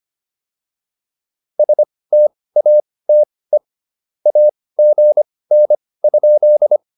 Morse code, heard as a keyed tone, about 25 wpm, spaced as for 18 wpm, 600 Hz. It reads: STATE AGN?
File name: POTA206 18